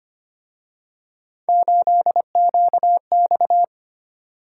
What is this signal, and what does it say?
Morse code, heard as a keyed tone, about 29 wpm, 700 Hz. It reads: 8QX
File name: Speed-Racing - Three Letter-Number Combinations - 25wpm